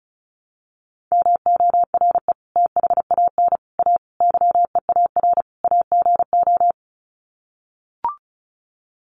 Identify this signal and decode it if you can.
Morse code, heard as a keyed tone, about 34 wpm, 700 Hz. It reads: MORE THAN A YEAR AGO
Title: Sets of 5 Words Encore2 35wpm